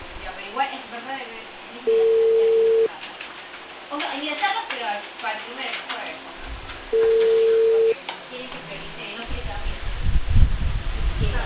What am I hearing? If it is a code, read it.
Ringback tone (China, 450 Hz), with speech.